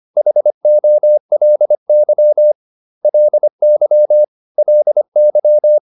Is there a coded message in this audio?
Yes: Morse code from 0.2 to 5.8 s, heard as a keyed tone, about 25 wpm, 600 Hz. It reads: HOLY LY LY